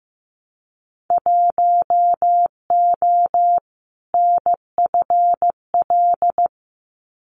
Morse code, heard as a keyed tone, about 15 wpm, 700 Hz. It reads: 1O NFL
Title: Winter Field Day Exchanges 15wpm